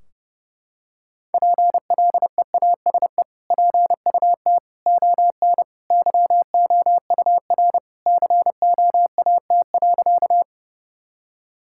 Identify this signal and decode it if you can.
Morse code, heard as a keyed tone, about 30 wpm, 700 Hz. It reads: PLEASE PUT ON YOUR COAT.
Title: Sentences from Top 1600 Words 30wpm